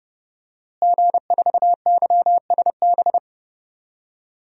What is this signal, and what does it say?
Morse code, heard as a keyed tone, about 29 wpm, 700 Hz. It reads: G4YSB
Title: DXCC-100 Call Signs - 5 Characters - Repeated 3x - 30wpm